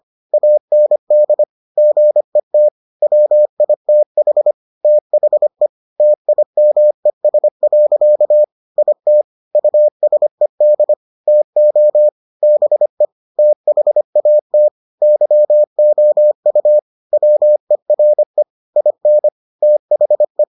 Morse code, heard as a keyed tone, about 25 wpm, 600 Hz. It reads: AND GET WITH THE TIMES. IT USED TO BE THAT YOU WERE IN THE